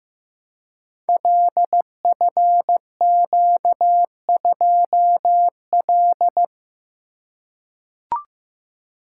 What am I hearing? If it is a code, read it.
Morse code, heard as a keyed tone, about 15 wpm, 700 Hz. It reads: LFQ2L
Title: Code Groups - 5 Letter:Numbers - 15wpm